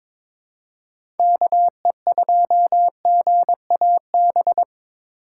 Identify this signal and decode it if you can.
Morse code, heard as a keyed tone, about 22 wpm, 700 Hz. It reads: KE2GAB
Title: USA Call Signs - 4x Character Spacing - 22wpm